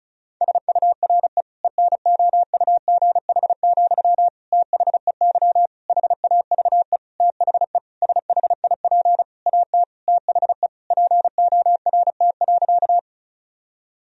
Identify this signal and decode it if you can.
Morse code, heard as a keyed tone, about 36 wpm, 700 Hz. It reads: SURE ENOUGH, THEY HAVE THE SHIP AT THE PORT.
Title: Sentences from Top 1000 Words - Review Part 2 - 35wpm